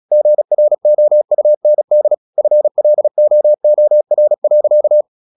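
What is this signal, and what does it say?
Morse code, heard as a keyed tone, about 36 wpm, 600 Hz. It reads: GROUND FLOOR.
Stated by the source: Morse practice at 35 wpm from CW med Gnister, sending Ken Follett's novel Never in English.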